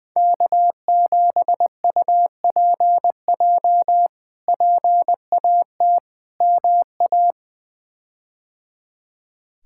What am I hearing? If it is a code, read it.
Morse code, heard as a keyed tone, about 20 wpm, 700 Hz. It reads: K7UPJ PAT MA
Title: K1USN SST QSO Practice - 20wpm